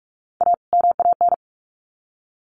Morse code, heard as a keyed tone, about 44 wpm, 700 Hz. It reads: A MAN
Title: Sets of 2 Words 45wpm